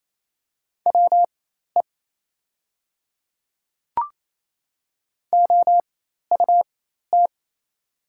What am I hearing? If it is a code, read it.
Morse code, heard as a keyed tone, about 28 wpm, spaced as for 13 wpm, 700 Hz. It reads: WE OUT
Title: 4x Character Spacing - Top 100 Words - 28wpm